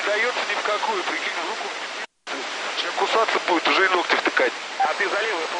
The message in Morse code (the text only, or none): none